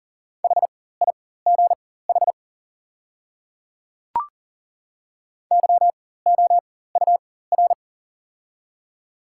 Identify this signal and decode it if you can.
Morse code, heard as a keyed tone, about 41 wpm, spaced as for 19 wpm, 700 Hz. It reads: HIGH YOUR
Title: 4x Character Spacing - Top 300 Words - 40wpm